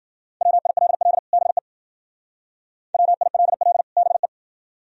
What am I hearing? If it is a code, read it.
Morse code, heard as a keyed tone, about 48 wpm, 700 Hz. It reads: WILL BE WILL BE
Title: Speed-Racing - Sets of 2 Words 40wpm